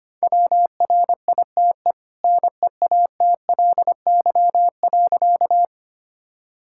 Morse code, heard as a keyed tone, about 25 wpm, 700 Hz. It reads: WRITE NEATLY.
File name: Easy Sentences with Hints - 25wpm